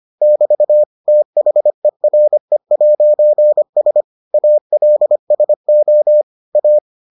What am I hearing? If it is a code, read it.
Morse code, heard as a keyed tone, about 25 wpm, 600 Hz. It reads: <BT> THERE'S ALSO A